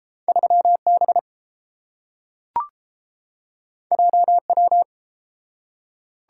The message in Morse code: 3B JW

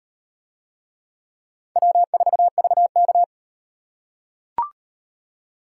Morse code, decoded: W4VK

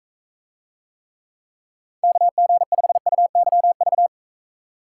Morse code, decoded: KG5UYV

